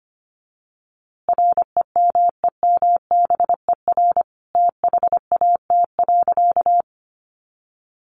Morse code, heard as a keyed tone, about 25 wpm, 700 Hz. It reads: REMEMBER THAT.
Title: A Sound Of Thunder 25wpm